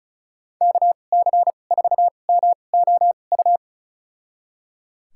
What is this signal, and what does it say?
Morse code, heard as a keyed tone, about 36 wpm, spaced as for 25 wpm, 700 Hz. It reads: KC4MOU
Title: USA Call Signs - 2x Character Spacing - 35wpm